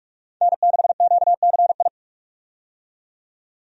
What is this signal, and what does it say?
Morse code, heard as a keyed tone, about 46 wpm, 700 Hz. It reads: N6QCI
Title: USA Call Signs - 5 Characters - 45wpm